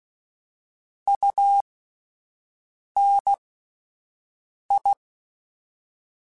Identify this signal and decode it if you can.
Morse code, heard as a keyed tone, about 16 wpm, spaced as for 5 wpm, 800 Hz. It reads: UNI